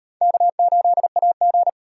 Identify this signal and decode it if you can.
Morse code, heard as a keyed tone, about 34 wpm, 700 Hz. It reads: K8AG